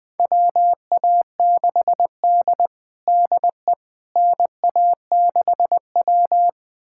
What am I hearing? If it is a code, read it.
Morse code, heard as a keyed tone, about 20 wpm, 700 Hz. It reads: WA6D DE NA6W
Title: USA Callsign DE Callsign - 4 Characters - 20wpm